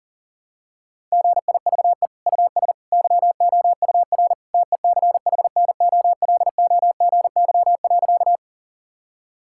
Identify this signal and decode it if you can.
Morse code, heard as a keyed tone, about 40 wpm, 700 Hz. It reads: GIVE US YOUR TECHNOLOGY.
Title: Sentences from Top 1000 Words - Review Part 5 - 40wpm